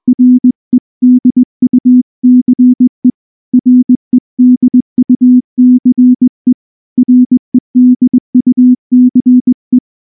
Morse code, heard as a keyed tone, about 20 wpm, 250 Hz. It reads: REDUCE REDUCE REDUCE
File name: morse / morse-training-20WPM